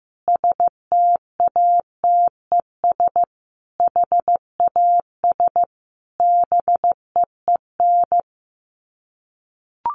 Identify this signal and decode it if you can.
Morse code, heard as a keyed tone, about 15 wpm, 700 Hz. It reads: STATES HAS BEEN
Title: Sets of 5 Words Encore2 15wpm